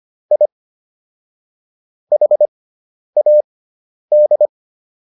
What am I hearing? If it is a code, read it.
Morse code, heard as a keyed tone, about 25 wpm, spaced as for 10 wpm, 600 Hz. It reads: I HAD